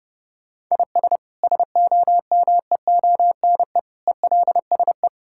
Morse code, heard as a keyed tone, about 30 wpm, 700 Hz. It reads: IS SOMEONE ELSE
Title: Sentences from Top 1000 Words - Review Part 5 - 30wpm